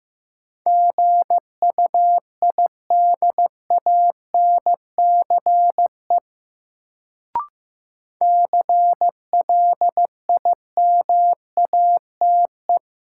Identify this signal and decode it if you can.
Morse code, heard as a keyed tone, about 15 wpm, 700 Hz. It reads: GUIDANCE CLIMATE
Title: Sentences from Top 1600 Words - New Word Drill Alt - 15wpm